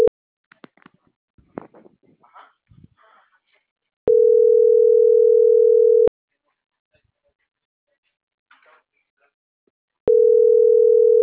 Ringback tone (North America), 440 + 480 Hz.